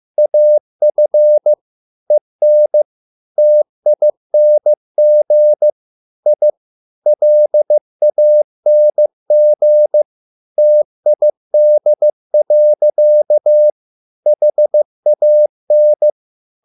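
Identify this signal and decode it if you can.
Morse code, heard as a keyed tone, about 15 wpm, 600 Hz. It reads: AF EN TING I LANG TID. HAN
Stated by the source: Morse code at a slow 15 wpm from CW med Gnister.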